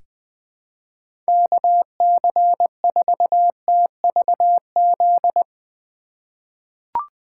Morse code, heard as a keyed tone, about 20 wpm, 700 Hz. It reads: KC4TVZ
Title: 2023 POTA Activator Call Signs - 20wpm